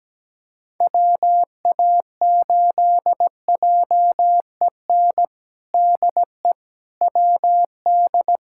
Morse code, heard as a keyed tone, about 17 wpm, 700 Hz. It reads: WA8JEN DE WD
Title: USA Callsign DE Callsign 17wpm